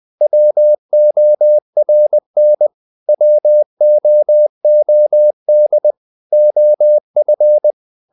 Morse code, heard as a keyed tone, about 20 wpm, 600 Hz. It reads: WORN WOOD OF
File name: the old man and the sea part 2 20wpm